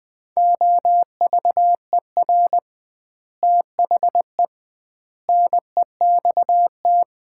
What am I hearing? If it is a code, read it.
Morse code, heard as a keyed tone, about 20 wpm, 700 Hz. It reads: OVER THE NEXT
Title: 2x Word-Spacing - Sets of 5 Words - 20wpm